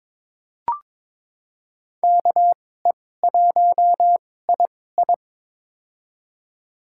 Morse code, heard as a keyed tone, about 23 wpm, spaced as for 16 wpm, 700 Hz. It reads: KE1II
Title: USA Call Signs - 5 Characters - 2x Character Spacing - 22wpm